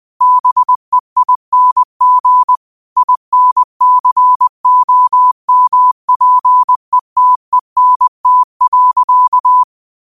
Morse code, heard as a keyed tone, about 20 wpm, 1000 Hz. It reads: BEING INCOMPETENT.
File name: qotdfiles / 20wpm